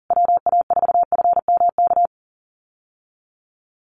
Morse code, heard as a keyed tone, about 40 wpm, 700 Hz. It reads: WA4FMK
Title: USA Call Signs Repeated 3x 40wpm